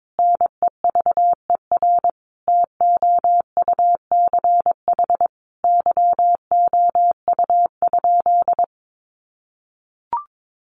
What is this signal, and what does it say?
Morse code, heard as a keyed tone, about 22 wpm, 700 Hz. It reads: NEVER TOUCH YOU?
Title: Sentences from Top 1200 Words 22wpm